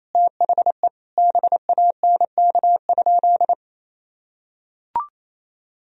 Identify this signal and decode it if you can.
Morse code, heard as a keyed tone, about 27 wpm, 700 Hz. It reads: THE BANK?
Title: Sentences from Top 1000 Words - Review Part 4 - 28wpm